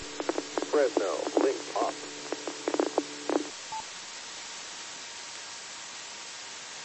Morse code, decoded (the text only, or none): none